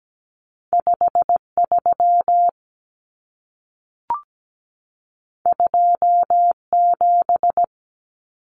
Morse code, heard as a keyed tone, about 17 wpm, 700 Hz. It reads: 53 27
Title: Common Cut Numbers - 2-Digit 17wpm